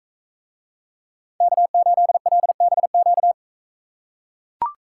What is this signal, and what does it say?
Morse code, heard as a keyed tone, about 44 wpm, 700 Hz. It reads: K8LBQ